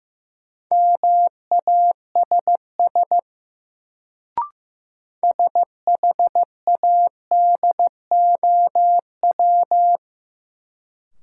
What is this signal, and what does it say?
Morse code, heard as a keyed tone, about 15 wpm, 700 Hz. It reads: MASS SHADOW
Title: Sets of 5 Words Encore - New Word Drill - 15wpm